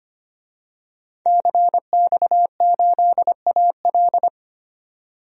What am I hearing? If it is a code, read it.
Morse code, heard as a keyed tone, about 25 wpm, 700 Hz. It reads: CX8AL
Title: DXCC-100 Call Signs - 5 Characters - 25wpm